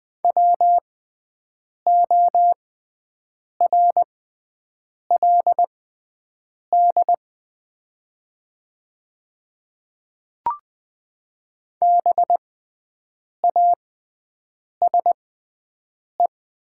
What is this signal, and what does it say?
Morse code, heard as a keyed tone, about 20 wpm, spaced as for 7 wpm, 700 Hz. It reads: WORLD BASE
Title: 6x Character Spacing - Top 300 Words - 20wpm